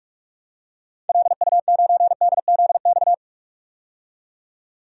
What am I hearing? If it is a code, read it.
Morse code, heard as a keyed tone, about 46 wpm, 700 Hz. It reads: LU9DZX
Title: DXCC-100 Call Signs - 6 Characters - 45wpm